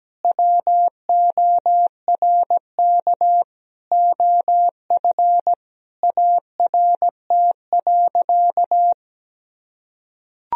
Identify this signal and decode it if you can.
Morse code, heard as a keyed tone, about 17 wpm, 700 Hz. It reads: WORK OF ART.
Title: Sentences from Top 1000 Words - Review Part 4 - 17wpm